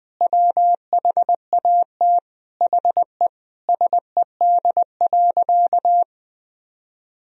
Morse code, heard as a keyed tone, about 20 wpm, 700 Hz. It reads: WHAT HE SED.
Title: Sentences from Top 400 Words and Abbreviations 20wpm